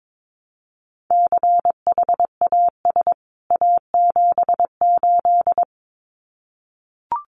Morse code, heard as a keyed tone, about 22 wpm, 700 Hz. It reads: CHAS A78
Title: CWT Exchange 22wpm